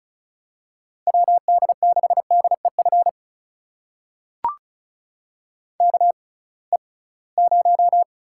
Morse code, heard as a keyed tone, about 36 wpm, spaced as for 23 wpm, 700 Hz. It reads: WD6DEF KE0
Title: USA Call Signs - 6 Characters - 6x Character Spacing - 35wpm